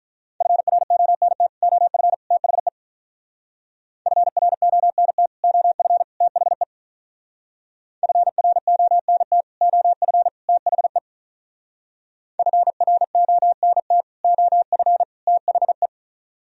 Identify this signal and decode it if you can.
Morse code, heard as a keyed tone, about 42 wpm, 700 Hz. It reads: FRONT OF THE FRONT OF THE FRONT OF THE FRONT OF THE
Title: Speed-Racing - Sets of 3 Words 35wpm